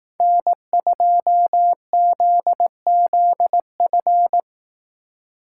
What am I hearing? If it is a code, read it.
Morse code, heard as a keyed tone, about 18 wpm, 700 Hz. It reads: N2ZZF
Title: Speed Racing - USA 1x3 Call Signs 15wpm